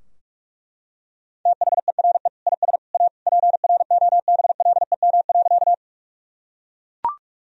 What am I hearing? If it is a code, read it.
Morse code, heard as a keyed tone, about 41 wpm, 700 Hz. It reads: THERE IS A PROBLEM.
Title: Sentences from Top 1000 Words - Review Part 2 - 45wpm